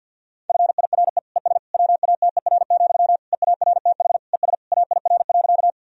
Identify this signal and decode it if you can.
Morse code, heard as a keyed tone, about 50 wpm, 700 Hz. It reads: FIRE ES WATER, EARTH ES AIR.
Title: Sentences from Top 400 Words and Abbreviations 50wpm